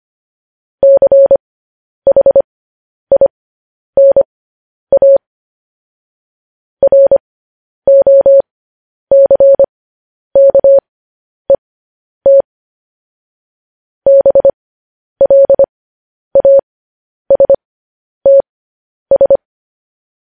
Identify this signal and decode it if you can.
Morse code, heard as a keyed tone, about 25 wpm, spaced as for 10 wpm, 550 Hz. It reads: CHINA ROCKET BLASTS